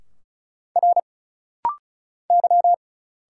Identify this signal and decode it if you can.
Morse code, heard as a keyed tone, about 34 wpm, 700 Hz. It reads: R Y
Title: Single Letters - Rapid-Fire 35wpm